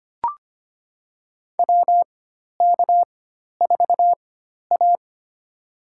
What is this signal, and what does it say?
Morse code, heard as a keyed tone, about 25 wpm, spaced as for 12 wpm, 700 Hz. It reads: WK4A
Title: USA Call Signs - 4 Characters - 4x Character Spacing - 25wpm